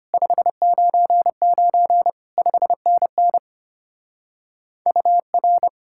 Morse code, heard as a keyed tone, about 31 wpm, 700 Hz. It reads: 599 5NN UR
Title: Speed-Racing Common QSO Phrases Encore - 25wpm